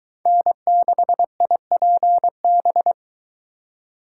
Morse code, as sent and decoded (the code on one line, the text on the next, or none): -. -.... .. .--. -...
N6IPB